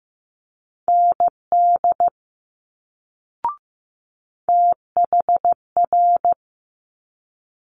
Morse code, heard as a keyed tone, about 15 wpm, 700 Hz. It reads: ND THR
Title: Multi-letter Phonemes Quick Fire 15wpm